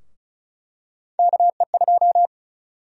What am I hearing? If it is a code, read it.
Morse code, heard as a keyed tone, about 36 wpm, 700 Hz. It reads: KE2
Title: USA Call Sign Prefixes - Repeated 2x - 35wpm